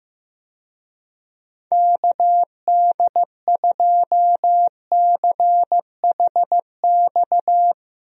Morse code, heard as a keyed tone, about 15 wpm, 700 Hz. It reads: KD2CHX
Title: USA Call Signs - 6 Characters - Repeated 2x - 15wpm